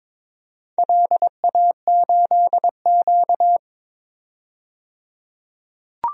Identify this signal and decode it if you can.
Morse code, heard as a keyed tone, about 22 wpm, 700 Hz. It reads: LA8Q
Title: DXCC-100 Call Signs - 4 Characters - 22wpm